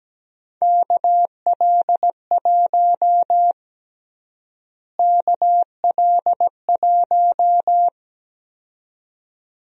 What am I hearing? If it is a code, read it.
Morse code, heard as a keyed tone, about 17 wpm, 700 Hz. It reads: KL1 KL1
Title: USA Call Sign Prefixes - Repeated 2x - 17wpm